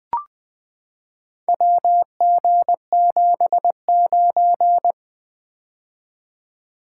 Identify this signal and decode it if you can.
Morse code, heard as a keyed tone, about 20 wpm, 700 Hz. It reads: WG79
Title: Four Letter-Number Combinations - 20wpm